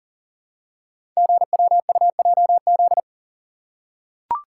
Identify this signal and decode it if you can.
Morse code, heard as a keyed tone, about 42 wpm, 700 Hz. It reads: GWUJZ
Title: Code Groups - 5 Letters - 40wpm